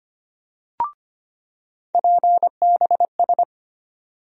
Morse code, heard as a keyed tone, about 25 wpm, 700 Hz. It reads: PBS
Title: USA Call Sign Postfixes - 3 Letters 25wpm